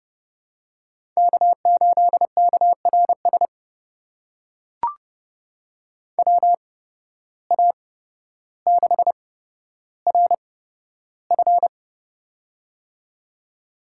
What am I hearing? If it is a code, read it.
Morse code, heard as a keyed tone, about 30 wpm, spaced as for 13 wpm, 700 Hz. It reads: K8KRS WA6RF
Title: USA Call Signs - 5 Characters - 8x Character Spacing - 30wpm